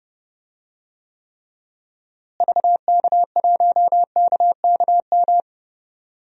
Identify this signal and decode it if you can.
Morse code, heard as a keyed tone, about 32 wpm, 700 Hz. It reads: VK1KKM
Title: DXCC-100 Call Signs - 6 Characters - Repeated 3x - 30wpm